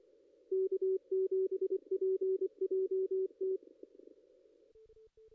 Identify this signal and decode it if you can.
Morse code, heard as a keyed tone, about 24 wpm, 380 Hz, with radio noise. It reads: K7PJT
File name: contests / ss-cw-k5tr-op-n5rz -2300